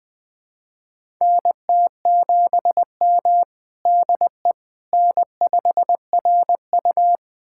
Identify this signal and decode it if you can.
Morse code, heard as a keyed tone, about 20 wpm, 700 Hz. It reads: NT7M DE N5RU
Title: USA Callsign DE Callsign - 4 Characters - 20wpm